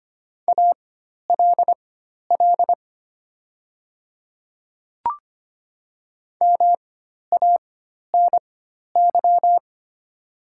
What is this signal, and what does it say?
Morse code, heard as a keyed tone, about 25 wpm, spaced as for 12 wpm, 700 Hz. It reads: ALL MANY
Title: 4x Character Spacing - Top 100 Words - 25wpm